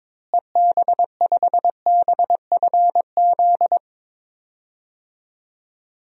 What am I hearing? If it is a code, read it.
Morse code, heard as a keyed tone, about 21 wpm, 700 Hz. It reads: EB5BFZ
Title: DXCC-100 Call Signs - 6 Characters - Repeated 2x - 22wpm